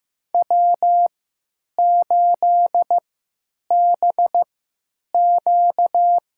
Morse code, heard as a keyed tone, about 16 wpm, spaced as for 9 wpm, 700 Hz. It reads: W8BQ
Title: USA Call Signs - 3x Character Spacing - 15wpm